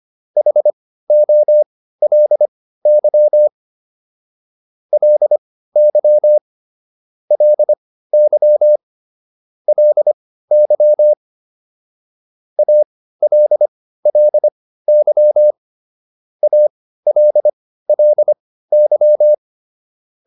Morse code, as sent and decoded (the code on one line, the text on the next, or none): .... --- .-.. -.-- / .-.. -.-- / .-.. -.-- / .-.. -.-- / .- .-.. .-.. -.-- / .- .-.. .-.. -.--
HOLY LY LY LY ALLY ALLY